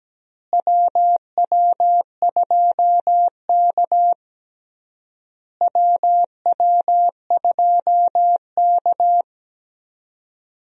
Morse code, heard as a keyed tone, about 17 wpm, 700 Hz. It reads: WW2K WW2K
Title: USA Call Signs - 4 Characters - Repeated 3x - 17wpm